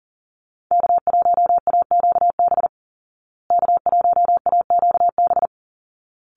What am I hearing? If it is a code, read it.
Morse code, heard as a keyed tone, about 40 wpm, 700 Hz. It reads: K1AQB K1AQB
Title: Call Signs Repeated 3x - 40wpm